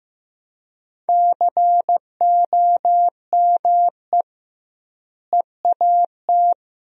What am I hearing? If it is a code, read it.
Morse code, heard as a keyed tone, about 15 wpm, 700 Hz. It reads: COME EAT
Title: 2x Word-Spacing - Sentences from Top 300 Words - 15wpm